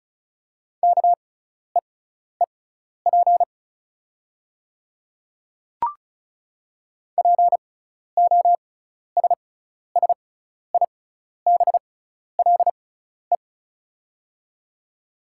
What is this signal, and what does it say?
Morse code, heard as a keyed tone, about 36 wpm, spaced as for 12 wpm, 700 Hz. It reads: KEEP POSSIBLE
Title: 6x Character Spacing - Top 500 Words - 35wpm